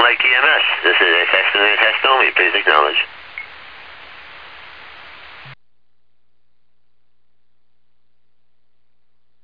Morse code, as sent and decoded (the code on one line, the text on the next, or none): none